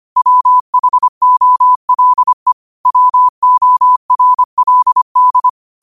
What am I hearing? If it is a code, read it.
Morse code, heard as a keyed tone, about 25 wpm, 1000 Hz. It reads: WHOLE WORLD
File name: qotdfiles / 25wpm